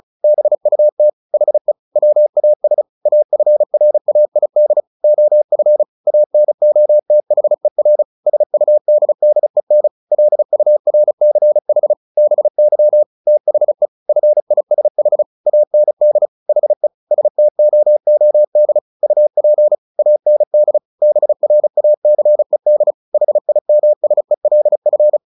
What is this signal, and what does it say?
Morse code, heard as a keyed tone, about 34 wpm, 600 Hz. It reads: BUT HE WAS AFRAID OF ANOTHER SUDDEN LURCH BY THE FISH AND HE STOOD UP AND BRACED HIMSELF
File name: the old man and the sea / the old man and the sea 1 35wpm